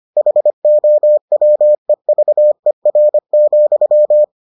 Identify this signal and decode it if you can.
Morse code, heard as a keyed tone, about 25 wpm, 600 Hz. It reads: HOWEVER,